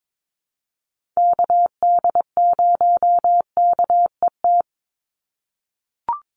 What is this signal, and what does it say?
Morse code, heard as a keyed tone, about 22 wpm, 700 Hz. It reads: KD0KET
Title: USA Call Signs 22wpm